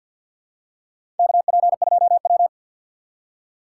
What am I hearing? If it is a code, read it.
Morse code, heard as a keyed tone, about 51 wpm, 700 Hz. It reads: KP2W
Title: USA Call Signs - 2x Character Spacing - 50wpm